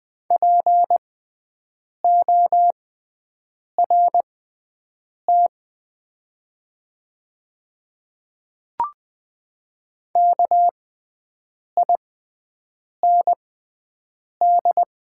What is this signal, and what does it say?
Morse code, heard as a keyed tone, about 20 wpm, spaced as for 7 wpm, 700 Hz. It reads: PORT KIND
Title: 6x Character Spacing - Top 300 Words - 20wpm